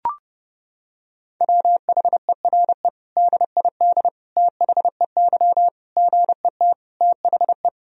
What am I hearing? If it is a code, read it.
Morse code, heard as a keyed tone, about 30 wpm, 700 Hz. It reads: WHERE DID THEY GET THE
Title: Sentences from Top 400 Words and Abbreviations 30wpm